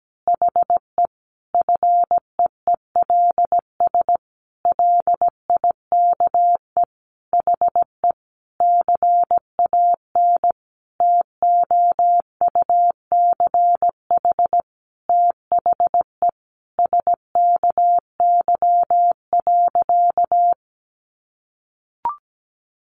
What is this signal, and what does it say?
Morse code, heard as a keyed tone, about 17 wpm, 700 Hz. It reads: HE FEELS LIKE HE CAN TOUCH THE SKY.
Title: Additional Easy Phrases 17wpm